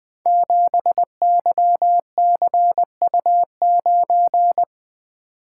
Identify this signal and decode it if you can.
Morse code, heard as a keyed tone, about 20 wpm, 700 Hz. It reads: 7YCU9
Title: Code Groups - 5 Letter:Numbers - 20wpm